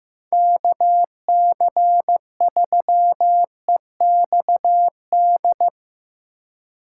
Morse code, heard as a keyed tone, about 15 wpm, 700 Hz. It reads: KC3EXD
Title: USA Call Signs - 3x Character Spacing - 15wpm